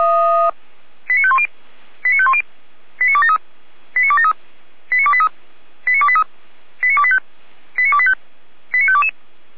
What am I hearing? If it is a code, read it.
Five-tone selective call (ZVEI), heard as quick runs of five tones, tones of about 70 ms. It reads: 89510 89510 89273 89273 89283 89283 89285 89285 89522